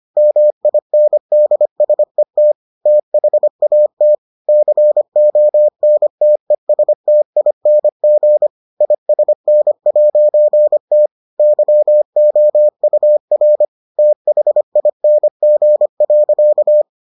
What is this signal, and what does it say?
Morse code, heard as a keyed tone, about 25 wpm, 600 Hz. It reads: MINDSET THAT CONTESTING ISN'T YOUR THING.